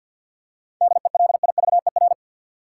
Morse code, heard as a keyed tone, about 51 wpm, 700 Hz. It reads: DELIVER
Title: Seven Letter Words - 50wpm